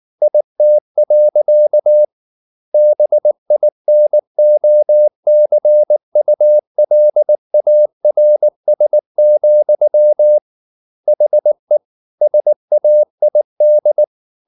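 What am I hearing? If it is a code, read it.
Morse code, heard as a keyed tone, about 19 wpm, 600 Hz. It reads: IT. BINOCULARS, HE SAID
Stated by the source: Morse code at 19 wpm from CW med Gnister, reading Ken Follett's novel Never in English.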